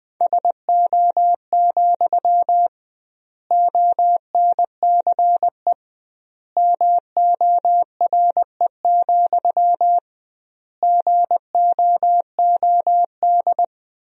Morse code, heard as a keyed tone, about 21 wpm, 700 Hz. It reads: SO, ONCE MORE, GOOD
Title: 2x Word-Spacing - Sentences from Top 300 Words - 20wpm